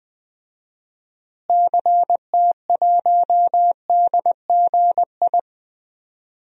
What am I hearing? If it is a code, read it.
Morse code, heard as a keyed tone, about 20 wpm, 700 Hz. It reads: CT1DGI
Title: DXCC-100 Call Signs - 6 Characters - Repeated 2x - 20wpm